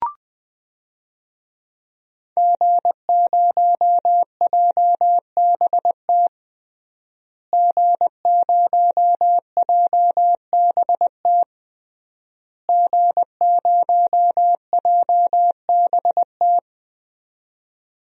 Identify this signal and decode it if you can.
Morse code, heard as a keyed tone, about 21 wpm, 700 Hz. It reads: G0JBT G0JBT G0JBT
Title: DXCC-100 Call Signs - 5 Characters - Repeated 3x - 20wpm